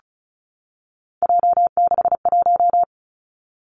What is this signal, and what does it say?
Morse code, heard as a keyed tone, about 36 wpm, 700 Hz. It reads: J61
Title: Three Challenging Letter-Number Combinations 35wpm